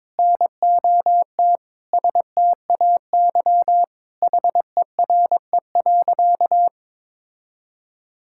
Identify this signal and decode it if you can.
Morse code, heard as a keyed tone, about 22 wpm, 700 Hz. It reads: NOT STAY HERE.
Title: Sentences from Top 1600 Words 22wpm